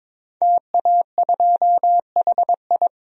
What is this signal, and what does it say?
Morse code, heard as a keyed tone, about 21 wpm, 700 Hz. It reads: TA2HI
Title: DXCC-100 Call Signs - 5 Characters - 22wpm